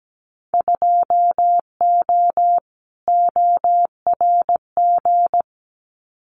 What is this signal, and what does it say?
Morse code, heard as a keyed tone, about 17 wpm, 700 Hz. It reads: 2O ORG
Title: Winter Field Day Exchanges 17wpm